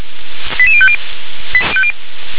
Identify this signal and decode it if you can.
Five-tone selective call (ZVEI), heard as quick runs of five tones, tones of about 70 ms. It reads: 88050
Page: /357/SDR/ZVEI/